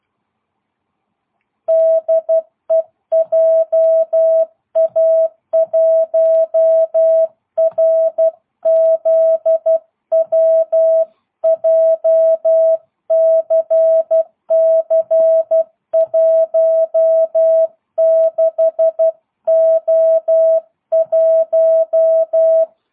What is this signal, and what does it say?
Morse code, heard as a keyed tone, about 12 wpm, 650 Hz. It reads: DE JA1RZW JCC16O1
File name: jh8cbh / 2022cw